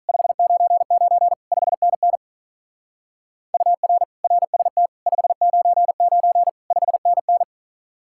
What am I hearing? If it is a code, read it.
Morse code, heard as a keyed tone, about 44 wpm, 700 Hz. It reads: 599 5NN UR RST 599 5NN